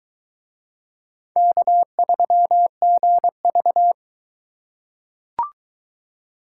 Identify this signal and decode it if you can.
Morse code, heard as a keyed tone, about 18 wpm, 700 Hz. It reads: K3GV
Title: Speed Racing - USA 1x2 Call Signs 15wpm